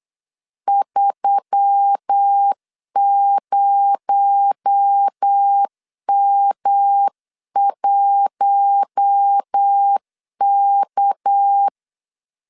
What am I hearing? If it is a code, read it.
Morse code, heard as a keyed tone, about 8 wpm, 800 Hz. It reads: 30M1K
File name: 8wpm